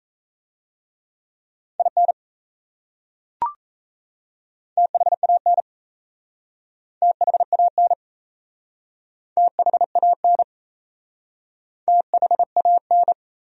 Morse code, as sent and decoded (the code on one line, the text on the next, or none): .. -. / - .... .- -. / - .... .- -. / - .... .- -. / - .... .- -.
IN THAN THAN THAN THAN